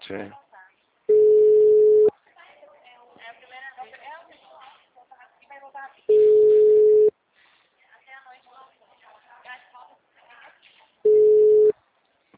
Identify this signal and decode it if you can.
Ringback tone (Europe, 425 Hz), with speech.